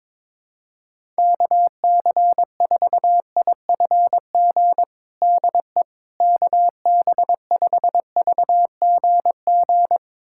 Morse code, heard as a keyed tone, about 21 wpm, 700 Hz. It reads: KC4IFG DE KB5VGG
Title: USA Callsign DE Callsign - 6 Characters - 22wpm